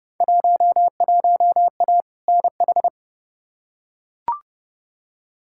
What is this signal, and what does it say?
Morse code, heard as a keyed tone, about 30 wpm, 700 Hz. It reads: 11A NH
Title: Field Day Exchanges Encore - 30wpm